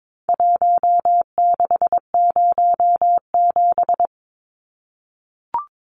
Morse code, heard as a keyed tone, about 22 wpm, 700 Hz. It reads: 1607
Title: Numbers - Contest Style - 4-Digit 22wpm